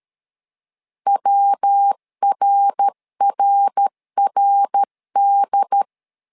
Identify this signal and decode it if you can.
Morse code, heard as a keyed tone, about 13 wpm, 800 Hz. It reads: WRRRD
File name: LEZIONI / 3 MWRO / 3 MWR 12wpm 1711